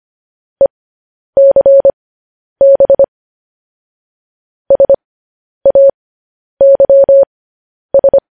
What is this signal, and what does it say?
Morse code, heard as a keyed tone, about 25 wpm, spaced as for 10 wpm, 550 Hz. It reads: ECB SAYS
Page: news-headlines-in-morse-code-at-10-wpm